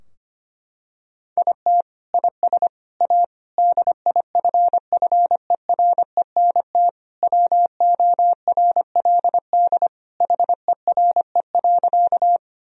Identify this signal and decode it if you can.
Morse code, heard as a keyed tone, about 25 wpm, 700 Hz. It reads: IT IS A DIFFERENT WORLD HERE.
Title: Sentences from Top 1000 Words - Review Part 3 - 25wpm